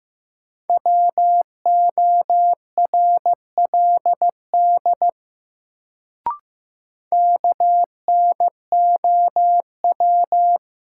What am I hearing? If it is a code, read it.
Morse code, heard as a keyed tone, about 15 wpm, 700 Hz. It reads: WORLD KNOW
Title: Sets of 2 Words - New Word Drill - 15wpm